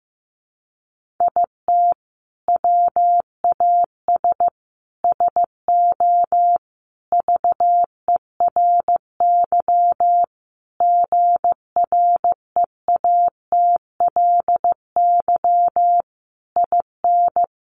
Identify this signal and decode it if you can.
Morse code, heard as a keyed tone, about 15 wpm, 700 Hz. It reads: IT WAS SO VERY GREATLY IN
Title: Sentences from Top 1400 Words 15wpm